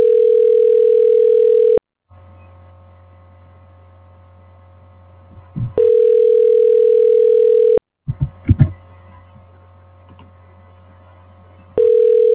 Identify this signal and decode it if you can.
Ringback tone (North America), 440 + 480 Hz.